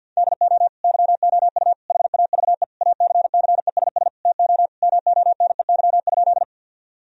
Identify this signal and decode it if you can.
Morse code, heard as a keyed tone, about 50 wpm, 700 Hz. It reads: DO YOU HAVE ACCESS TO MONEY?
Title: Sentences from Top 1000 Words - Review Part 4 - 50wpm